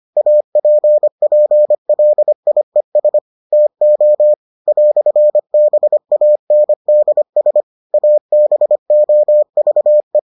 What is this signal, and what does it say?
Morse code, heard as a keyed tone, about 25 wpm, 600 Hz. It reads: APPLIES TO "BANDS ABOVE